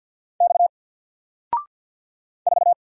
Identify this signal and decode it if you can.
Morse code, heard as a keyed tone, about 48 wpm, 700 Hz. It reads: X 4
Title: Single Letter-Number Alt 50wpm